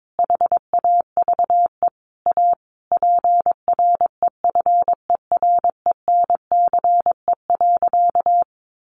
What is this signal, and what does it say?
Morse code, heard as a keyed tone, about 22 wpm, 700 Hz. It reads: HAVE A PREFERENCE.
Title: Sentences from Top 1500 Words 22wpm